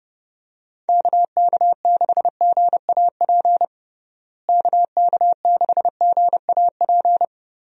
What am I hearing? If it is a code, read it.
Morse code, heard as a keyed tone, about 32 wpm, 700 Hz. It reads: KK6GAP KK6GAP
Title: USA Call Signs - 6 Characters - Repeated 3x - 30wpm